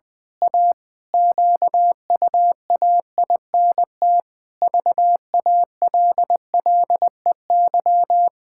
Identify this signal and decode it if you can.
Morse code, heard as a keyed tone, about 20 wpm, 700 Hz. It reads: A QUAINT VALLEY
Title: Pangram Challenge v2 - 20wpm